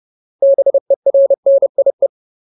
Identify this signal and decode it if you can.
Morse code, heard as a keyed tone, about 30 wpm, 550 Hz. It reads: BERNIE
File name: Ultimate Word List / 30WPM Ultimate